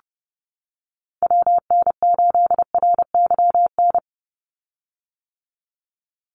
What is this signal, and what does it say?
Morse code, heard as a keyed tone, about 30 wpm, 700 Hz. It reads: WN8RYN
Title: USA Call Signs 30wpm